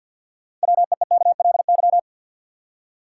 Morse code, heard as a keyed tone, about 52 wpm, 700 Hz. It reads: WEEKLY
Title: Six Letter Words - 50wpm